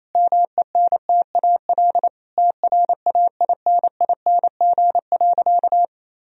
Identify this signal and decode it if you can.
Morse code, heard as a keyed tone, about 28 wpm, 700 Hz. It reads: MENTAL TRAINING.
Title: Sentences from Top 1000 Words - Review Part 4 - 28wpm